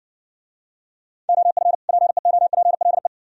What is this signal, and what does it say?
Morse code, heard as a keyed tone, about 67 wpm, 700 Hz. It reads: OF PEOPLE